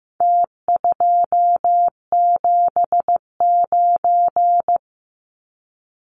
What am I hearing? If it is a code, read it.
Morse code, heard as a keyed tone, about 15 wpm, 700 Hz. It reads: T279
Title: Numbers - Contest Style - 4-Digit 15wpm